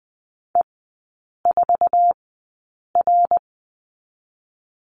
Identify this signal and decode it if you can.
Morse code, heard as a keyed tone, about 20 wpm, spaced as for 8 wpm, 700 Hz. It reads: E4R
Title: 081 Lesson 14e - Number 2 - Three Characters 20wpm